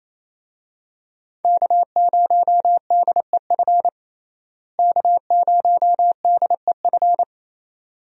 Morse code, heard as a keyed tone, about 28 wpm, 700 Hz. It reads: K0DEF K0DEF
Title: USA Call Signs - 5 Characters - Repeated 2x - 28wpm